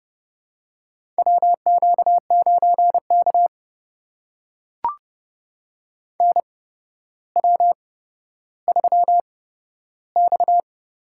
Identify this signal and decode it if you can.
Morse code, heard as a keyed tone, about 30 wpm, spaced as for 13 wpm, 700 Hz. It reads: WQ9K NW3X
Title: USA Call Signs - 4 Characters - 8x Character Spacing - 30wpm